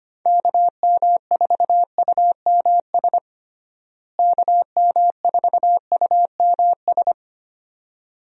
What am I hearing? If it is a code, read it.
Morse code, heard as a keyed tone, about 25 wpm, 700 Hz. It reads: KM4UMS KM4UMS
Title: USA Call Signs - 6 Characters - Repeated 2x - 25wpm